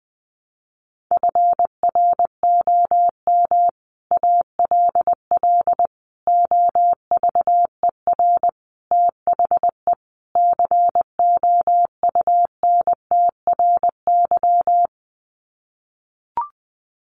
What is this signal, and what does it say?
Morse code, heard as a keyed tone, about 20 wpm, 700 Hz. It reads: FROM ALL OVER THE COUNTRY
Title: Sets of 5 Words Encore 20wpm